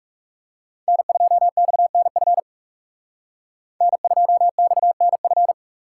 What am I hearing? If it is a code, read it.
Morse code, heard as a keyed tone, about 39 wpm, 700 Hz. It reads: N2XNF N2XNF